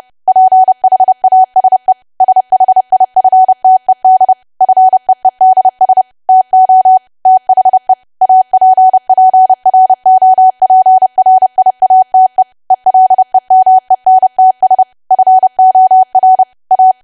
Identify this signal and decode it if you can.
Morse code, heard as a keyed tone, about 30 wpm, 750 Hz. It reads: PHASE SHIFTED FEEDS TO THE APPROPRIATE ELEMENTS FOR A
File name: cwa / W1AW-Level3 / 30WPM